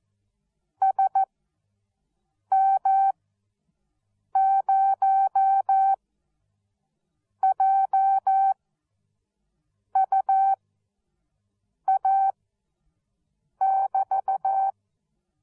Morse code, heard as a keyed tone, about 14 wpm, spaced as for 5 wpm, 780 Hz. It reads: SM0JUA<BT>
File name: Onlinelehrgang / morsen / DJ4UF Morsen13